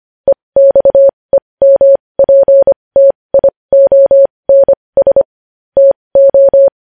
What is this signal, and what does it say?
Morse code, heard as a keyed tone, about 25 wpm, spaced as for 20 wpm, 550 Hz. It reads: EXEMPTIONS TO
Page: News Headlines in Morse Code at 20 WPM